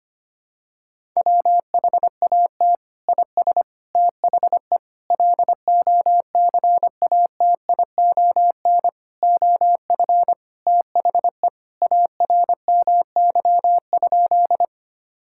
Morse code, heard as a keyed tone, about 25 wpm, 700 Hz. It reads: WHAT IS THE LOCATION OF THE ARMY?
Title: Sentences from Top 1000 Words - Review Part 6 - 25wpm